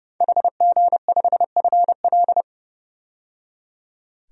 Morse code, heard as a keyed tone, about 32 wpm, 700 Hz. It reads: HG5FL
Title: DXCC-100 Call Signs - 5 Characters - 30wpm